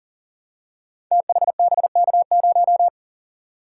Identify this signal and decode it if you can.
Morse code, heard as a keyed tone, about 38 wpm, 690 Hz. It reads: THBK0
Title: Code Groups - 5 Letter:Numbers - 40wpm